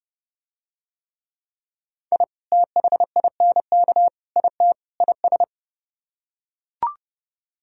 Morse code, heard as a keyed tone, about 29 wpm, 700 Hz. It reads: I THINK IT IS